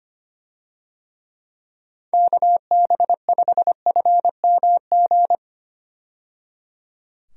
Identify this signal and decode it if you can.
Morse code, heard as a keyed tone, about 25 wpm, 700 Hz. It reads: KB5FMG
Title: USA Call Signs - 6 Characters - 25wpm